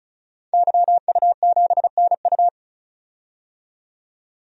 Morse code, heard as a keyed tone, about 36 wpm, 700 Hz. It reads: YU7NU
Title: DXCC-100 Call Signs - 5 Characters - 35wpm